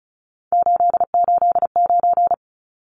Morse code, heard as a keyed tone, about 34 wpm, 700 Hz. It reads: 889
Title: Numbers - 3-Digit 35wpm